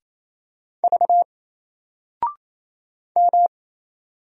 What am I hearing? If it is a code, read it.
Morse code, heard as a keyed tone, about 27 wpm, 700 Hz. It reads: V M
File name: Single-Letter-Number-28wpm